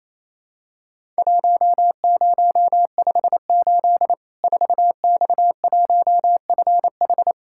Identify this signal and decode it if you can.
Morse code, heard as a keyed tone, about 27 wpm, 700 Hz. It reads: 1058 4X1FH
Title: Worked All Europe DX Contest - QTC Exchange - 28wpm